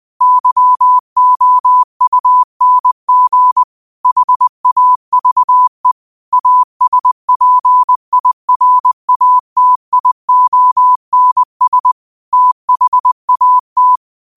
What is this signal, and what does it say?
Morse code, heard as a keyed tone, about 20 wpm, 1000 Hz. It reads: YOUNG HAVE ASPIRATIONS THAT